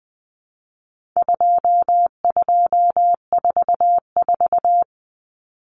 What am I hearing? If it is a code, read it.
Morse code, heard as a keyed tone, about 20 wpm, 700 Hz. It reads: 2244
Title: Common Cut Numbers - 4-Digit 20wpm